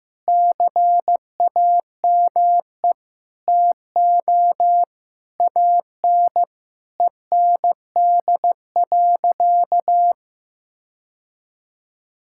Morse code, heard as a keyed tone, about 15 wpm, 700 Hz. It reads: CAME TO AN END.
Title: Sentences from Top 1600 Words 15wpm